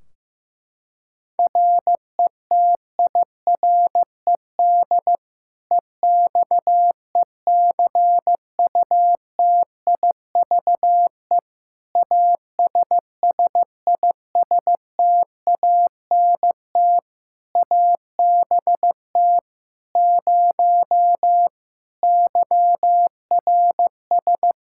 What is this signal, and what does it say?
Morse code, heard as a keyed tone, about 15 wpm, 700 Hz. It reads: RETIRED EXECUTIVE ASSISTANT ABT 0 YRS
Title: K1USN SST QSO Practice 15wpm